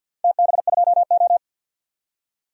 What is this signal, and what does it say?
Morse code, heard as a keyed tone, about 49 wpm, 700 Hz. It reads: TB2O
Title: DXCC-100 Call Signs - 4 Characters - 50wpm